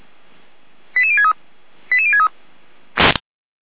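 Five-tone selective call (ZVEI), heard as quick runs of five tones, tones of about 70 ms. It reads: 88063 88063
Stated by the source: ZVEI five-tone selective call received on an SDR.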